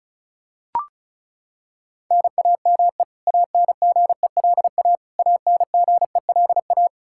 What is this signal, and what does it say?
Morse code, heard as a keyed tone, about 36 wpm, 700 Hz. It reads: NAME ANGELA ANGELA
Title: Practice QSO - Exchange 1 - Beginning - 35wpm